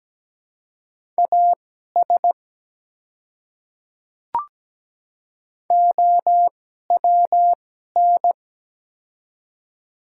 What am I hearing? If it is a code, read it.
Morse code, heard as a keyed tone, about 17 wpm, spaced as for 12 wpm, 700 Hz. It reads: AS OWN
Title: 2x Character Spacing - Top 300 Words - 17wpm